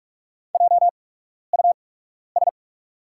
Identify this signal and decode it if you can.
Morse code, heard as a keyed tone, about 44 wpm, spaced as for 12 wpm, 690 Hz. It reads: JUS
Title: 8x Character Spacing - Top 200 Words - 45wpm